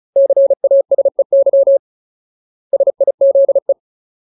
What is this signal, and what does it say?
Morse code, heard as a keyed tone, about 34 wpm, 550 Hz. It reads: CASEY SIZE
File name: Ultimate Word List / 35WPM Ultimate